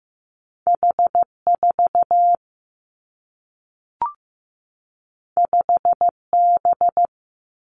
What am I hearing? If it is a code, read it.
Morse code, heard as a keyed tone, about 15 wpm, 700 Hz. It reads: H 4 5 B
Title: Two Challenging Letter-Number Combinations 15wpm